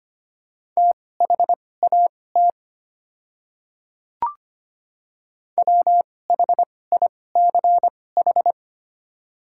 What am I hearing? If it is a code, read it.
Morse code, heard as a keyed tone, about 25 wpm, spaced as for 18 wpm, 700 Hz. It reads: THAT WHICH
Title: 2x Character Spacing - Top 100 Words - 25wpm